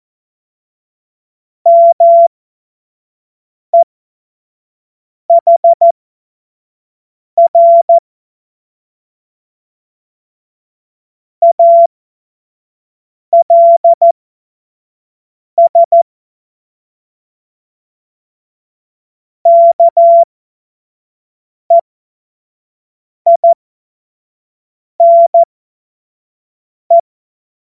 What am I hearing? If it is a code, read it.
Morse code, heard as a keyed tone, about 14 wpm, spaced as for 5 wpm, 680 Hz. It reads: MEHR ALS KEINE
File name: Morsekurs / Witze / Witz 05. April 2022 25 F70